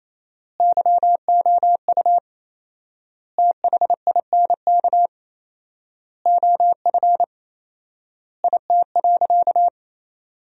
Morse code, heard as a keyed tone, about 28 wpm, 700 Hz. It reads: YOU THINK OF IT.
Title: 4x Word-Spacing - Sentences from Top 200 Words - 28wpm